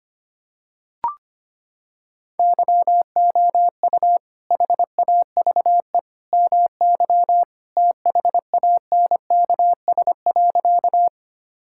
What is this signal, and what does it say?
Morse code, heard as a keyed tone, about 25 wpm, 700 Hz. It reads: YOU HAVE MY THANKS.
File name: practice / 25wpm / Sentences with 2-Word Phrases - 25wpm